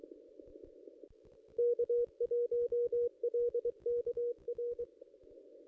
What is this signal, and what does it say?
Morse code, heard as a keyed tone, about 23 wpm, 480 Hz. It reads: K1LKR
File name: contests / ss-cw-k5tr-op-n5rz -1800